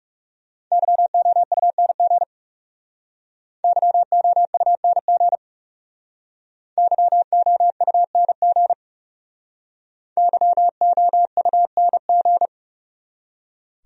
Morse code, heard as a keyed tone, about 40 wpm, 700 Hz. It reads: YOUNG YOUNG YOUNG YOUNG